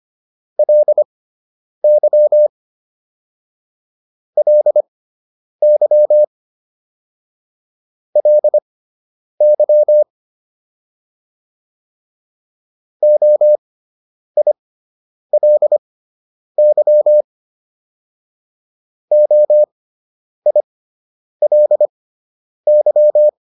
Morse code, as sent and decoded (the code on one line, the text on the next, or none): .-.. -.-- / .-.. -.-- / .-.. -.-- / --- .. .-.. -.-- / --- .. .-.. -.--
LY LY LY OILY OILY